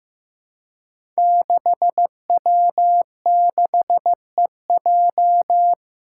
Morse code, heard as a keyed tone, about 15 wpm, 700 Hz. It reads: 6W6EJ
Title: Five Challenging Letter-Number Combinations 15wpm